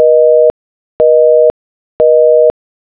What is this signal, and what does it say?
Busy tone (North America), 480 + 620 Hz, with noise.
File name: be / Busy tone